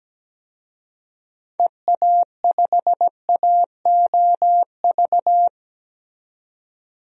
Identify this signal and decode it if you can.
Morse code, heard as a keyed tone, about 17 wpm, 700 Hz. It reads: EA5AOV
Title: DXCC-100 Call Signs - 6 Characters - 17wpm